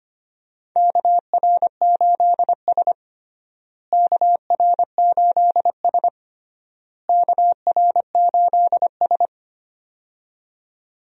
Morse code, heard as a keyed tone, about 25 wpm, 700 Hz. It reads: KR8S KR8S KR8S
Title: USA Call Signs - 4 Characters - Repeated 3x - 25wpm